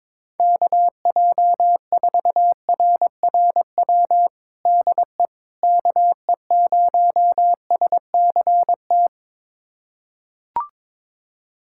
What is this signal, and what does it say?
Morse code, heard as a keyed tone, about 22 wpm, 700 Hz. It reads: KJ4RRW DE KE0SCT
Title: USA Callsign DE Callsign - 6 Characters - 22wpm